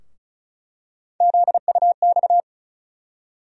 Morse code, heard as a keyed tone, about 36 wpm, 700 Hz. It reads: ZUX
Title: USA Call Sign Postfixes - 3 Letters 35wpm